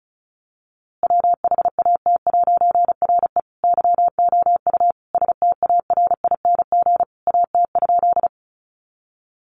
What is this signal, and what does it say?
Morse code, heard as a keyed tone, about 34 wpm, 700 Hz. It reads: WHAT'RE YOU STARING AT?
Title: A Sound Of Thunder 35wpm